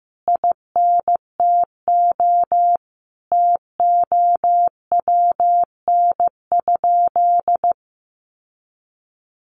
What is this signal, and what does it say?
Morse code, heard as a keyed tone, about 15 wpm, 700 Hz. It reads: INTO TOWN?
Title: Sentences from Top 1400 Words 15wpm